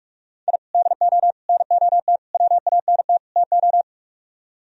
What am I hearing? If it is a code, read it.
Morse code, heard as a keyed tone, about 45 wpm, 700 Hz. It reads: I DO NOT WANT TO